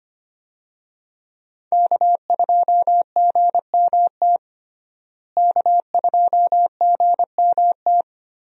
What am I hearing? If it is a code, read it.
Morse code, heard as a keyed tone, about 25 wpm, 700 Hz. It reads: K2GMT K2GMT
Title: USA Call Signs - 5 Characters - Repeated 3x - 25wpm